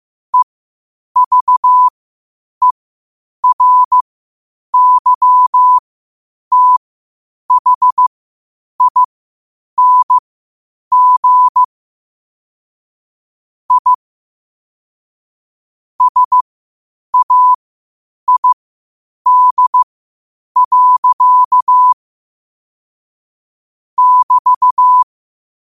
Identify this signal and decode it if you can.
Morse code, heard as a keyed tone, about 15 wpm, spaced as for 8 wpm, 1000 Hz. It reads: EVERYTHING I SAID. <BT>
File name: qotdfiles / 8wpm